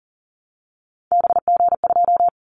Morse code, heard as a keyed tone, about 40 wpm, 700 Hz. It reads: BG2
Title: Callsign Prefixes for Top 100 DXCC - 40wpm